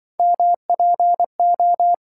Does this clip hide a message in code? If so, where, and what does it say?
Yes: Morse code from 0.2 to 1.9 s, heard as a keyed tone, about 27 wpm, 700 Hz. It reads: MPO